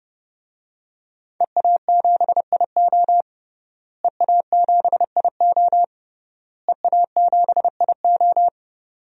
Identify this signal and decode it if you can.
Morse code, heard as a keyed tone, about 29 wpm, 700 Hz. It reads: EA7IO EA7IO EA7IO
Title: DXCC-100 Call Signs - 5 Characters - Repeated 3x - 30wpm